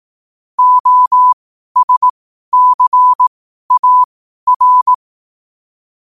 Morse code, heard as a keyed tone, about 18 wpm, spaced as for 12 wpm, 1000 Hz. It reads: OSCAR
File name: qotdfiles / 12wpm